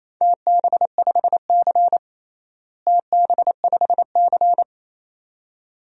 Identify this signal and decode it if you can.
Morse code, heard as a keyed tone, about 28 wpm, 700 Hz. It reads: TB5C TB5C
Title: DXCC-100 Call Signs - 4 Characters - Repeated 2x - 28wpm